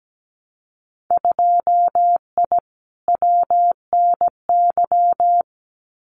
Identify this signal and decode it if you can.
Morse code, heard as a keyed tone, about 17 wpm, 700 Hz. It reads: 2I WNY
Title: Winter Field Day Exchanges 17wpm